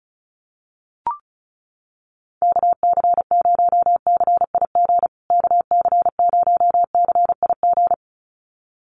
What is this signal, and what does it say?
Morse code, heard as a keyed tone, about 34 wpm, 700 Hz. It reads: KC0CIG KC0CIG
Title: USA Call Signs Repeated 2x 35wpm